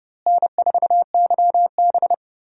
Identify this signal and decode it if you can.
Morse code, heard as a keyed tone, about 30 wpm, 700 Hz. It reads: N4YB
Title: USA Call Signs - 4 Characters - Repeated 3x - 30wpm